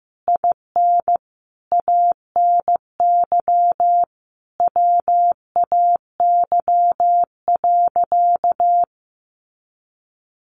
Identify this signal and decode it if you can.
Morse code, heard as a keyed tone, about 15 wpm, 700 Hz. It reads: IN ANY WAY.
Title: Sentences from Top 1100 Words 15wpm